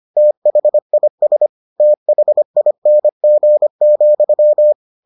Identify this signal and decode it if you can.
Morse code, heard as a keyed tone, about 25 wpm, 600 Hz. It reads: THIS THING,